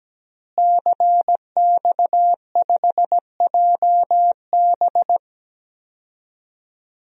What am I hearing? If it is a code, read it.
Morse code, heard as a keyed tone, about 17 wpm, 700 Hz. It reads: CX5JB
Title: DXCC-100 Call Signs - 5 Characters - Repeated 3x - 17wpm